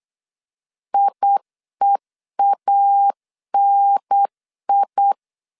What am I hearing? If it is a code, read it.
Morse code, heard as a keyed tone, about 8 wpm, 800 Hz. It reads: IEANI